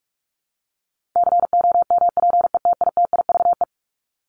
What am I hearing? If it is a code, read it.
Morse code, heard as a keyed tone, about 44 wpm, 700 Hz. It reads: COMPETITIVE
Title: Top 1001-1250 Words 45wpm